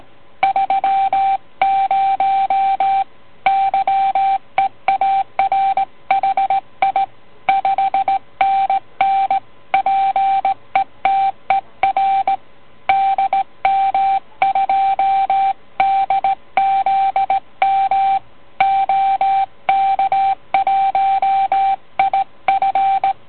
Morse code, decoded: 30 YEAR HI 5NN PETER DM2DZM OK1IF